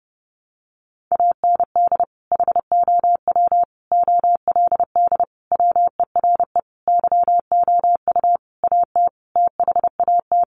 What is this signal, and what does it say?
Morse code, heard as a keyed tone, about 30 wpm, 700 Hz. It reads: AND HOW OLD WERE YOU AT THAT
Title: Sentences from Top 200 Words 30wpm